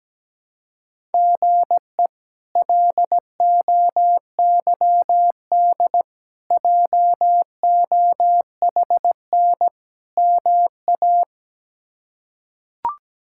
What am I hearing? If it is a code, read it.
Morse code, heard as a keyed tone, about 17 wpm, 700 Hz. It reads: GE LOYD JOHN MA
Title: K1USN SST QSO Practice - 17wpm